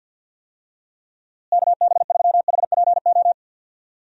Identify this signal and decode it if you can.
Morse code, heard as a keyed tone, about 50 wpm, 700 Hz. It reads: KB3HPO
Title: USA Call Signs - 6 Characters - Repeated 3x - 50wpm